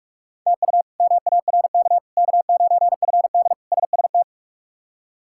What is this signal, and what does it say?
Morse code, heard as a keyed tone, about 43 wpm, 700 Hz. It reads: TU MARK K9FD SST
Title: K1USN SST QSO Practice - 45wpm